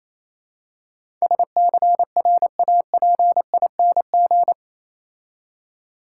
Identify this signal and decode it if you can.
Morse code, heard as a keyed tone, about 27 wpm, 690 Hz. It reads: SCRAPING
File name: Easy Words - Speed-Racing with Hints - 22wpm